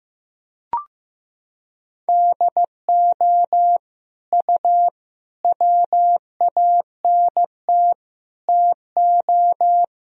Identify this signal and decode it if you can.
Morse code, heard as a keyed tone, about 15 wpm, 700 Hz. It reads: DO U WANT TO
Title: Sentences from Top 300 Words and Abbreviations 15wpm